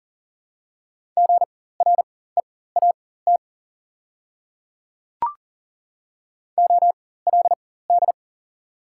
Morse code, heard as a keyed tone, about 40 wpm, spaced as for 19 wpm, 690 Hz. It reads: GREAT OLD
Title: 4x Character Spacing - Top 200 Words - 40wpm